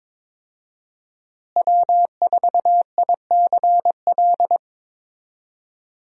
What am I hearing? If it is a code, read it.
Morse code, heard as a keyed tone, about 22 wpm, 700 Hz. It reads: W4ICL
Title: USA Call Signs - 5 Characters - Repeated 3x - 22wpm